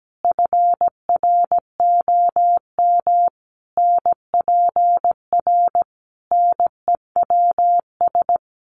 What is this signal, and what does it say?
Morse code, heard as a keyed tone, about 17 wpm, 700 Hz. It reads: FROM NPR NEWS
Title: Sets of 5 Words 17wpm